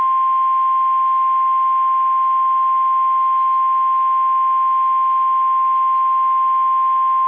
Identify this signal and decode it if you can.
Emergency Alert System (SAME) tones (the 1050 Hz attention tone). It is heard from 0.8 to 6.4 s.
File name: eas / KSDZKDJL / KSDZKDJL